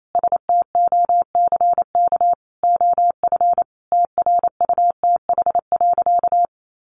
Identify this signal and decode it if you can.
Morse code, heard as a keyed tone, about 28 wpm, 700 Hz. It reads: STOCK OF TRUTH.
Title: Sentences from Top 1500 Words 28wpm